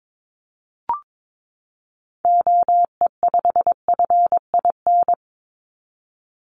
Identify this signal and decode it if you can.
Morse code, heard as a keyed tone, about 22 wpm, 700 Hz. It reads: OE5FIN
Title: Call Signs - 22wpm